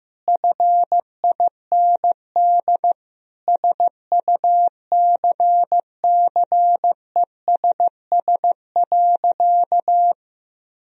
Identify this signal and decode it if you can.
Morse code, heard as a keyed tone, about 15 wpm, 700 Hz. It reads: FIND SUCCESS.
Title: Sentences from Top 1000 Words - Review Part 4 - 15wpm